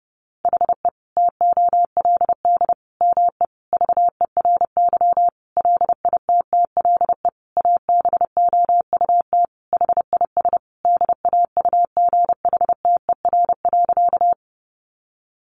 Morse code, heard as a keyed tone, about 30 wpm, 700 Hz. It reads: HE TOLD ME VERY LITTLE ABOUT HIS DAUGHTER.
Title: Sentences from Top 1500 Words 30wpm